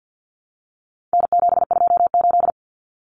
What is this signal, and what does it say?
Morse code, heard as a keyed tone, about 48 wpm, 700 Hz. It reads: N728
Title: Common Cut Numbers - 4-Digit 50wpm